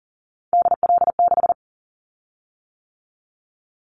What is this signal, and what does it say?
Morse code, heard as a keyed tone, about 40 wpm, 700 Hz. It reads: DL6
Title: Callsign Prefixes for Top 25 DXCC - 40wpm